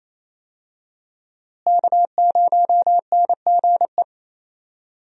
Morse code, heard as a keyed tone, about 28 wpm, 700 Hz. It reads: K0NGE
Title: USA Call Signs - 5 Characters - Repeated 2x - 28wpm